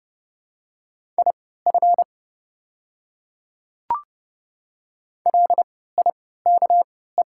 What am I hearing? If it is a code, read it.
Morse code, heard as a keyed tone, about 30 wpm, spaced as for 17 wpm, 700 Hz. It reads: IF LIKE
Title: 3x Character Spacing - Top 200 Words - 30wpm